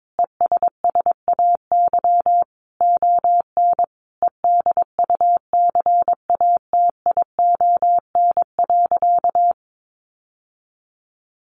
Morse code, heard as a keyed tone, about 22 wpm, 700 Hz. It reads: ESSAY ON EDUCATION.
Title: Sentences from Top 1500 Words 22wpm